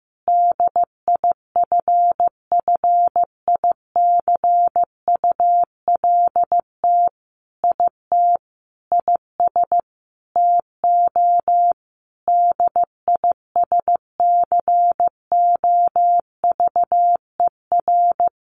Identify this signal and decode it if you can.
Morse code, heard as a keyed tone, about 15 wpm, 700 Hz. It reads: DIFFICULT IT IS TO DISCOVER
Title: Sentences from Top 1100 Words 15wpm